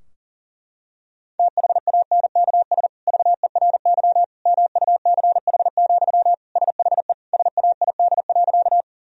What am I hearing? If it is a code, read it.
Morse code, heard as a keyed tone, about 40 wpm, 700 Hz. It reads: THANKS VERY MUCH, SHE SAID.
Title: Sentences from Top 1000 Words - Review Part 3 - 40wpm